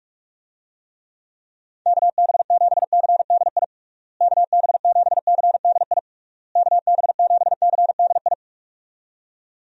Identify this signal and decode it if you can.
Morse code, heard as a keyed tone, about 41 wpm, 700 Hz. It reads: KB7CDI KB7CDI KB7CDI
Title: USA Call Signs - 6 Characters - Repeated 3x - 45wpm